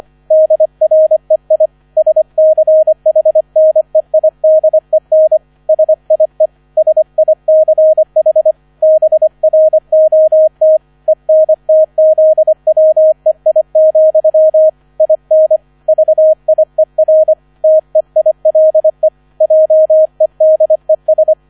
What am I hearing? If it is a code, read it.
Morse code, heard as a keyed tone, about 25 wpm, 630 Hz, with radio noise. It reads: DREI SCHNEIDEN SIE SICH BROT ENTZWEI, IN VIER TEILE JEDES